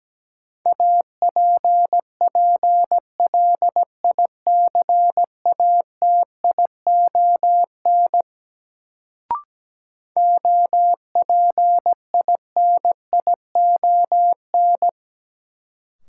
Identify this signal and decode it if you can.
Morse code, heard as a keyed tone, about 17 wpm, 700 Hz. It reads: APPLICATION OPINION
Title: Sentences from Top 1200 Words - New Word Drill - 17wpm